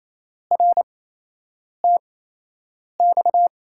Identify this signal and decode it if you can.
Morse code, heard as a keyed tone, about 28 wpm, spaced as for 8 wpm, 700 Hz. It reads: RTX
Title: USA Call Signs - 5 Characters - 8x Character Spacing - 28wpm